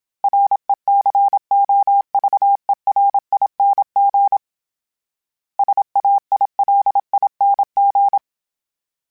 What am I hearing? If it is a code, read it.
Morse code, heard as a keyed tone, about 27 wpm, 800 Hz. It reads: RECOVERING SAILING
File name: Special INGs-mp3s / Special INGs-mp3s / Sngl Ing @25